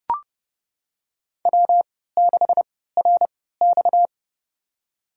Morse code, heard as a keyed tone, about 31 wpm, spaced as for 17 wpm, 700 Hz. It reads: W6RX
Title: USA Call Signs - 3x Character Spacing - 30wpm